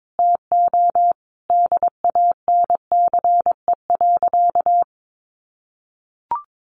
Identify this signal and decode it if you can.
Morse code, heard as a keyed tone, about 22 wpm, 700 Hz. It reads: TO DANCE.
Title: Sentences from Top 700 Words 22wpm